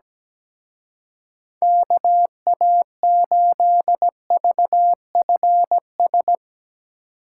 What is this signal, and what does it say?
Morse code, heard as a keyed tone, about 17 wpm, 700 Hz. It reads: KA8VFS
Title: Speed Racing - USA 2x3 Call Signs 17wpm